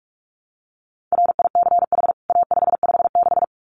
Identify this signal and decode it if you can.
Morse code, heard as a keyed tone, about 44 wpm, 700 Hz. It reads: RICH A556
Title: CWT Exchange 45wpm